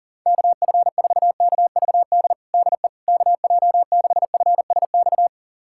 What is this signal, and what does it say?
Morse code, heard as a keyed tone, about 37 wpm, 690 Hz. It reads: KF4KVD DE KJ6FSX
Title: USA Callsign DE Callsign - 6 Characters - 40wpm